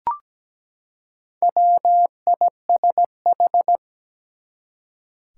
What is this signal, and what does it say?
Morse code, heard as a keyed tone, about 17 wpm, 700 Hz. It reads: WISH
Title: Four Letter Words - 17wpm